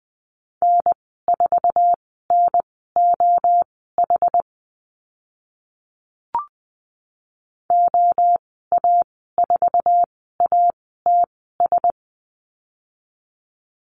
Morse code, heard as a keyed tone, about 20 wpm, spaced as for 15 wpm, 700 Hz. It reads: N4NOH OA4ATS
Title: 069 Lesson 12g - Letter d - Call Signs 20wpm